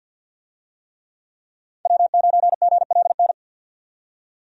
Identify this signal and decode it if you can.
Morse code, heard as a keyed tone, about 51 wpm, 700 Hz. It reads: W9GLN
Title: USA Call Signs - 5 Characters - 50wpm